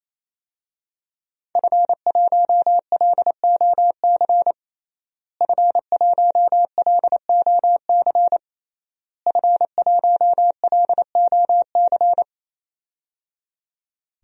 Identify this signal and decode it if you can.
Morse code, heard as a keyed tone, about 28 wpm, 700 Hz. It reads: F1LOC F1LOC F1LOC
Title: DXCC-100 Call Signs - 5 Characters - Repeated 3x - 28wpm